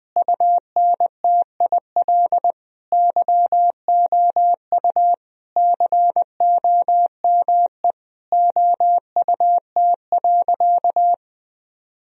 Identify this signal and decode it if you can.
Morse code, heard as a keyed tone, about 20 wpm, 700 Hz. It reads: UNTIL YOU COME OUT.
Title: Sentences from Top 1000 Words - Review Part 1 - 20wpm